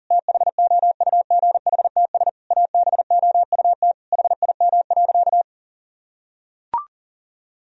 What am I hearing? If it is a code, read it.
Morse code, heard as a keyed tone, about 40 wpm, 700 Hz. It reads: THOUGHTS ABOUT HIM.
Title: Sentences from Top 1000 Words - Review Part 1 - 40wpm